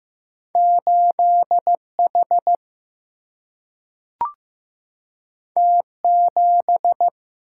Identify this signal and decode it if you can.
Morse code, heard as a keyed tone, about 15 wpm, 700 Hz. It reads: 8 H T 7
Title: Single-Letter Single-Number 15wpm